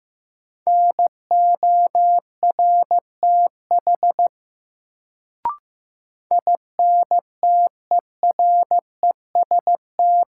Morse code, heard as a keyed tone, about 15 wpm, 700 Hz. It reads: NORTH INTEREST
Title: Sentences from Top 400 Words - New Word Drill - 15wpm